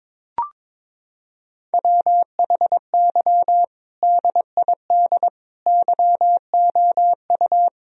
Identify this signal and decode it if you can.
Morse code, heard as a keyed tone, about 23 wpm, 700 Hz. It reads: WHY DID YOU
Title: Sentences from Top 1000 Words - Review Part 3 - 22wpm